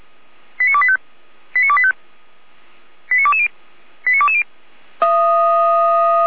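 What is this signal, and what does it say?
Five-tone selective call (ZVEI), heard as quick runs of five tones, tones of about 70 ms. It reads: 89285 89285 89220 89220
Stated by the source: ZVEI five-tone selective call received on an SDR, logged by Feuerwehr Orlamünde fire brigade.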